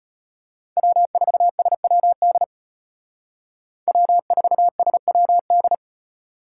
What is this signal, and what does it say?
Morse code, heard as a keyed tone, about 32 wpm, 700 Hz. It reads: W4SWD W4SWD